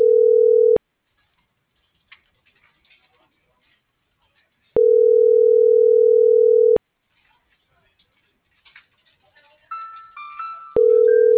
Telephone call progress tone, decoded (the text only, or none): ringback tone